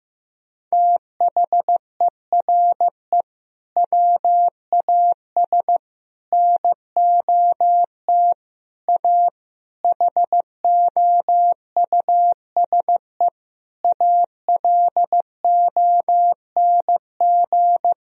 Morse code, heard as a keyed tone, about 15 wpm, 700 Hz. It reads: THERE WAS NOT A HOUSE ALONG
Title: Sentences from Top 1000 Words - Review Part 2 - 15wpm